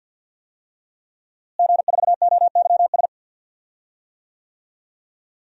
Morse code, heard as a keyed tone, about 51 wpm, 700 Hz. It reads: G4OYS
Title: DXCC-100 Call Signs - 5 Characters - Repeated 2x - 50wpm